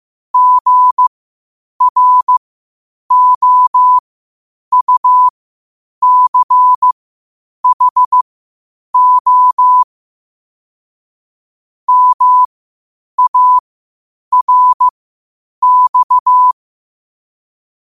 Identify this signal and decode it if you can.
Morse code, heard as a keyed tone, about 15 wpm, spaced as for 8 wpm, 1000 Hz. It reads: GROUCHO MARX
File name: qotdfiles / 8wpm